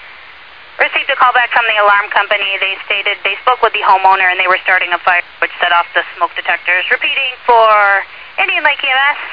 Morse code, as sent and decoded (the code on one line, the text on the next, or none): none